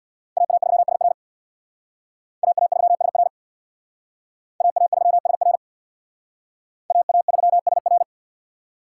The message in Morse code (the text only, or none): AA3SR AA3SR AA3SR AA3SR